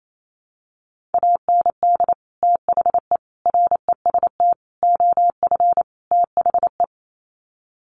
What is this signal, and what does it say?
Morse code, heard as a keyed tone, about 28 wpm, 700 Hz. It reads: AND THE REST OF THE
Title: Sets of 5 Words 28wpm